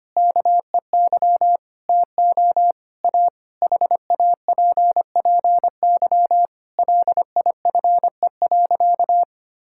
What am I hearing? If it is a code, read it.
Morse code, heard as a keyed tone, about 25 wpm, 700 Hz. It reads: KEY TO A HAPPY LIFE.
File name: practice / 25wpm / Difficult Sentences with Hints - 25wpm